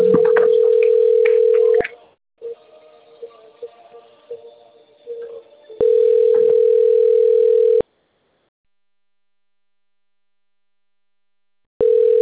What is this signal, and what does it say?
Ringback tone (North America), 440 + 480 Hz.
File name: rg-1600-unknown- - - .17048